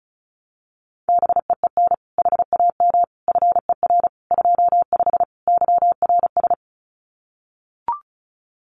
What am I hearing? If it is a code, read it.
Morse code, heard as a keyed tone, about 34 wpm, 700 Hz. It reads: BEEN HAM FER 25 YRS
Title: Common QSO Phrases Encore 35wpm